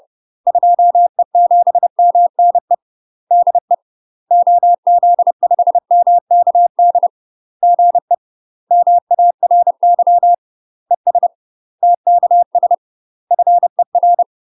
Morse code, heard as a keyed tone, about 30 wpm, 700 Hz. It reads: 2E7MNE DE OZ5MKD GE MARY ES TKS FER